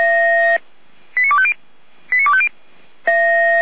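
Five-tone selective call (ZVEI), heard as quick runs of five tones, tones of about 70 ms. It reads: 89250 89250
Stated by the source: ZVEI five-tone call received by SDR, from audio.f85360.feuerwehr-orlamuende.org (fire brigade Orlamünde).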